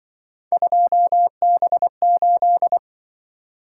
Morse code, heard as a keyed tone, about 24 wpm, 700 Hz. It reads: 2B8
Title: Speed-Racing - Three Letter-Number Combinations - 20wpm